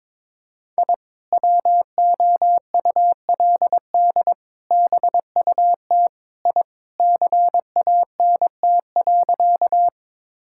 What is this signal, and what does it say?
Morse code, heard as a keyed tone, about 22 wpm, 700 Hz. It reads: I WOULD BUT I CANT.
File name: practice / 22wpm / Sentences with 2-Word Phrases - 22wpm